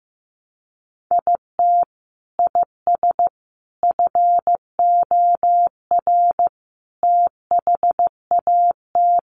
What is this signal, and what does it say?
Morse code, heard as a keyed tone, about 15 wpm, 700 Hz. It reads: IT IS FOR THAT
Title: Sentences from Top 700 Words 15wpm